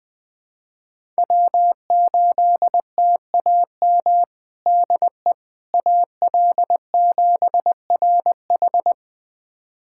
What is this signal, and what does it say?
Morse code, heard as a keyed tone, about 21 wpm, 700 Hz. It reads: W8TAM DE AL7RH
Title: USA Callsign DE Callsign - 5 Characters - 20wpm